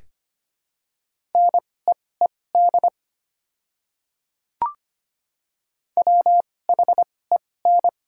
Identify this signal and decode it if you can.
Morse code, heard as a keyed tone, about 25 wpm, spaced as for 18 wpm, 700 Hz. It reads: NEED WHEN
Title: 2x Character Spacing - Top 300 Words - 25wpm